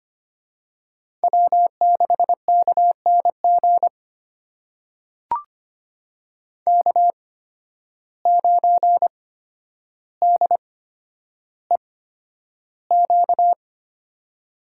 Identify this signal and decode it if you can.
Morse code, heard as a keyed tone, about 25 wpm, spaced as for 11 wpm, 700 Hz. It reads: W6KNG K9DEQ
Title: USA Call Signs - 5 Characters - 8x Character Spacing - 25wpm